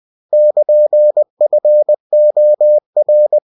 Morse code, heard as a keyed tone, about 20 wpm, 600 Hz. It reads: <KN>FOR
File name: cw-books / winnie-the-pooh / winnie-20wpm-page-0103